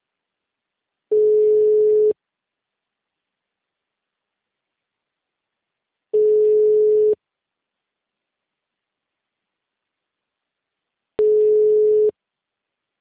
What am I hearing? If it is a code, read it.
Ringback tone (Europe, 425 Hz).
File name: out- -2003- - - .22431